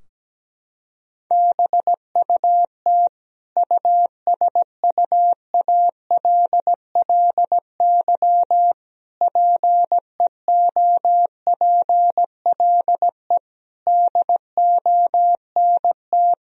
Morse code, heard as a keyed tone, about 17 wpm, 700 Hz. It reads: BUT USUALLY PEOPLE DONT
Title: Sentences from Top 1000 Words - Review Part 3 - 17wpm